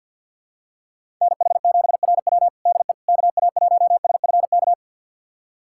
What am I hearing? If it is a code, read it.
Morse code, heard as a keyed tone, about 50 wpm, 700 Hz. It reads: NH7RW DE KA1SFK
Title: USA Callsign DE Callsign 50wpm